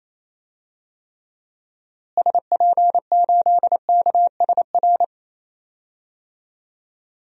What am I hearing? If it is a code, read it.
Morse code, heard as a keyed tone, about 28 wpm, 700 Hz. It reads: SP8KSR
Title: DXCC-100 Call Signs - 6 Characters - Repeated 3x - 28wpm